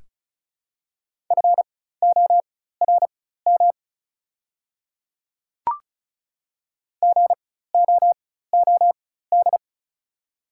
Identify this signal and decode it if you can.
Morse code, heard as a keyed tone, about 36 wpm, spaced as for 16 wpm, 700 Hz. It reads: FORM GOOD
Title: 4x Character Spacing - Top 200 Words - 35wpm